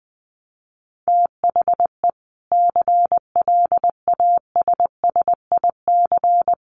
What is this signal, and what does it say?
Morse code, heard as a keyed tone, about 20 wpm, 700 Hz. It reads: THE CLASSIC
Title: Sentences from Top 1000 Words 20wpm